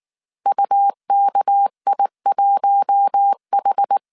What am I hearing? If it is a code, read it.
Morse code, heard as a keyed tone, about 19 wpm, 800 Hz. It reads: UKI1H
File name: CW / DET / 1 9 1500C S A 18wpm 2459